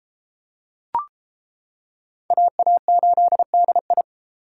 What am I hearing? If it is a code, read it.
Morse code, heard as a keyed tone, about 33 wpm, 700 Hz. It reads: AA8DI